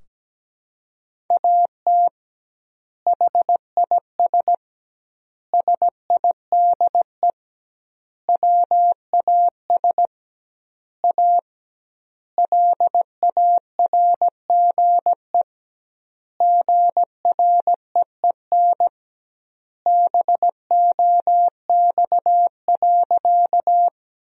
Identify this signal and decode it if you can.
Morse code, heard as a keyed tone, about 17 wpm, 700 Hz. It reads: AT HIS SIDE WAS A LARGE GREEN BOX.
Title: 2x Word-Spacing - Sentences from Top 500 Words - 17wpm